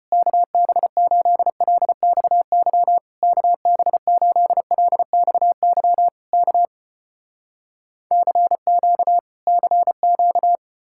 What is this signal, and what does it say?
Morse code, heard as a keyed tone, about 33 wpm, 700 Hz. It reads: KB8LXY KB8LXY K CQ CQ